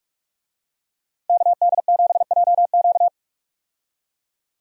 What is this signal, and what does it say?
Morse code, heard as a keyed tone, about 46 wpm, 700 Hz. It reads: KD7JQ
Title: USA Call Signs - 5 Characters - 45wpm